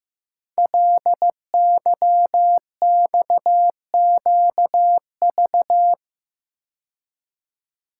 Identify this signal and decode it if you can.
Morse code, heard as a keyed tone, about 15 wpm, 700 Hz. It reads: LYXQV
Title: Code Groups - 5 Letters - 15wpm